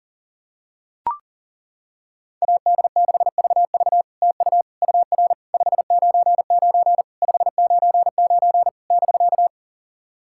Morse code, heard as a keyed tone, about 37 wpm, 660 Hz. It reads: AD6VV TU UR 599 599 <BK>
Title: SOTA QSO Practice 40wpm